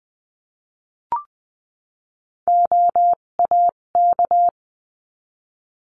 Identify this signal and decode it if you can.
Morse code, heard as a keyed tone, about 20 wpm, spaced as for 17 wpm, 700 Hz. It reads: OAK
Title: 257 Lesson 42b - Words 14-20wpm